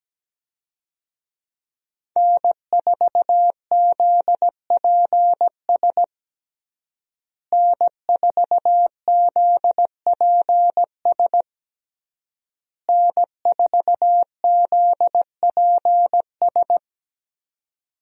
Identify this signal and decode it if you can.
Morse code, heard as a keyed tone, about 17 wpm, 700 Hz. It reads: N4ZPS N4ZPS N4ZPS
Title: USA Call Signs - 5 Characters - Repeated 3x - 17wpm